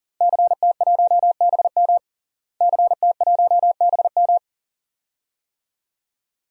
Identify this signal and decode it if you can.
Morse code, heard as a keyed tone, about 40 wpm, 700 Hz. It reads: CT1BM CT1BM
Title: DXCC-100 Call Signs - 5 Characters - Repeated 2x - 40wpm